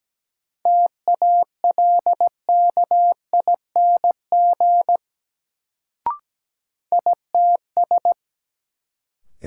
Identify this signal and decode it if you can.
Morse code, heard as a keyed tone, about 17 wpm, 700 Hz. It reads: TALKING ITS
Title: Sets of 2 Words - New Word Drill - 17wpm